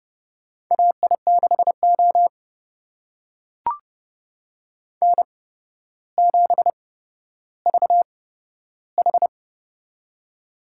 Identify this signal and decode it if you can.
Morse code, heard as a keyed tone, about 29 wpm, spaced as for 13 wpm, 700 Hz. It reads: AI6O N7VH
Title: USA Call Signs - 4 Characters - 8x Character Spacing - 30wpm